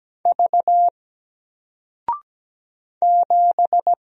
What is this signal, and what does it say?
Morse code, heard as a keyed tone, about 17 wpm, 700 Hz. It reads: V 7
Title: Single Letter-Number Alt 17wpm